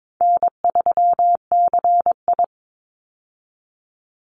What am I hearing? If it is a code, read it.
Morse code, heard as a keyed tone, about 22 wpm, 700 Hz. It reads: N3CI
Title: Easy Call Signs Repeated 2x - 22wpm